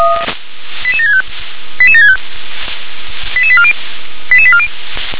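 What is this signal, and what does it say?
Five-tone selective call (ZVEI), heard as quick runs of five tones, tones of about 70 ms. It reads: 88765 88765 88840 88840
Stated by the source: ZVEI five-tone selective call 88962 received via SDR.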